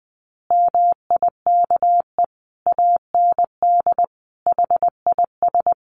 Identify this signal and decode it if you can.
Morse code, heard as a keyed tone, about 20 wpm, 700 Hz. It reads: MIKE AND HIS
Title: Additional Easy Phrases 20wpm